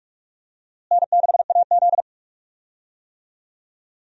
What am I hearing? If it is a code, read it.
Morse code, heard as a keyed tone, about 45 wpm, 700 Hz. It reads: N6AZ
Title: USA Call Signs - 4 Characters - Repeated 2x - 45wpm